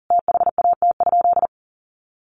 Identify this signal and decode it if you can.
Morse code, heard as a keyed tone, about 40 wpm, 700 Hz. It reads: THAT?
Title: Sentences from Top 1200 Words 40wpm